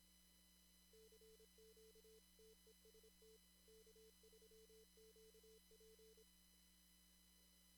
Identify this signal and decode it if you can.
Morse code, heard as a keyed tone, about 26 wpm, 440 Hz. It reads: CQ TEST K3QP
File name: CW / kt5j wpx cw-op-k5pi